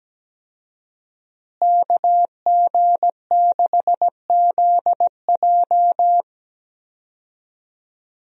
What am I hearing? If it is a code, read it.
Morse code, heard as a keyed tone, about 17 wpm, 700 Hz. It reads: KG6ZJ
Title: USA Call Signs - 5 Characters - Repeated 3x - 17wpm